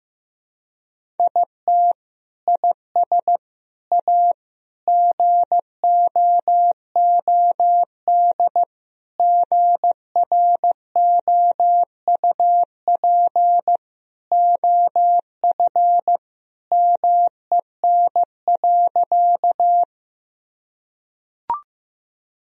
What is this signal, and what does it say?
Morse code, heard as a keyed tone, about 15 wpm, 700 Hz. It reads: IT IS A GOOD GROUP OF MEN.
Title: Sentences from Top 1000 Words - Review Part 1 - 15wpm